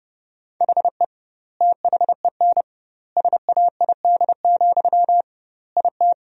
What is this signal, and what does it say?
Morse code, heard as a keyed tone, about 30 wpm, 700 Hz. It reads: HE THEN SAID, IT
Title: 2x Word-Spacing - Sentences from Top 300 Words - 30wpm